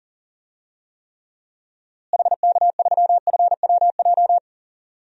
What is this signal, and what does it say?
Morse code, heard as a keyed tone, about 40 wpm, 700 Hz. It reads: HK3FWJ
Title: DXCC-100 Call Signs - 6 Characters - Repeated 2x - 40wpm